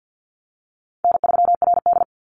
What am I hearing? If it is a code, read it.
Morse code, heard as a keyed tone, about 50 wpm, 700 Hz. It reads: N3RD
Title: Easy Call Signs - 50wpm